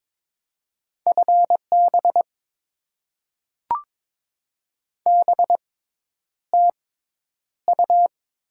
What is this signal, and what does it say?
Morse code, heard as a keyed tone, about 22 wpm, spaced as for 10 wpm, 690 Hz. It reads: FB BTU
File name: Top 20 QSO Elements - 6x Character Spacing - 22wpm